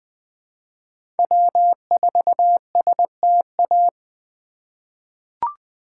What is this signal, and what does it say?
Morse code, heard as a keyed tone, about 21 wpm, 700 Hz. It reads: W4STA
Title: USA Call Signs - 5 Characters - 2x Character Spacing - 20wpm